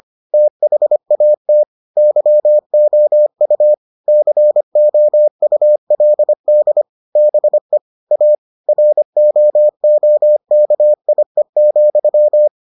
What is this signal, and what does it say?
Morse code, heard as a keyed tone, about 25 wpm, 600 Hz. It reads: THAT YOU COULD BE A ROOKIE,